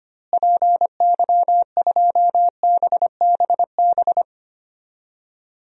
Morse code, heard as a keyed tone, about 25 wpm, 700 Hz. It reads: PY2BBB
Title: DXCC-100 Call Signs - 6 Characters - Repeated 2x - 25wpm